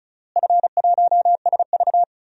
Morse code, heard as a keyed tone, about 34 wpm, 700 Hz. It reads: F1SV
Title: DXCC-100 Call Signs - 4 Characters - Repeated 3x - 35wpm